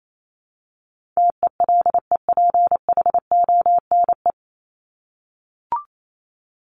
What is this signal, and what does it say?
Morse code, heard as a keyed tone, about 28 wpm, 700 Hz. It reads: TELEPHONE
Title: Top 1501-1750 Words 28wpm